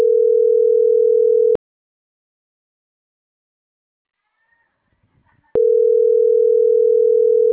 Ringback tone (North America), 440 + 480 Hz.